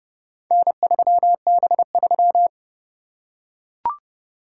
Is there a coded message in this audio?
Yes: Morse code from 0.5 to 2.5 s, heard as a keyed tone, about 30 wpm, 700 Hz. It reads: N3B3